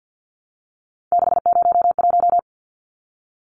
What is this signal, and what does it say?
Morse code, heard as a keyed tone, about 50 wpm, 700 Hz. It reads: 601
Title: Numbers - 3-Digit 50wpm